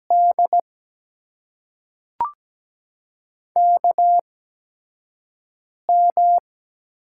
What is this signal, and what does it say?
Morse code, heard as a keyed tone, about 17 wpm, spaced as for 4 wpm, 700 Hz. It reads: DKM
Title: USA Call Signs - 6 Characters - 8x Character Spacing - 17wpm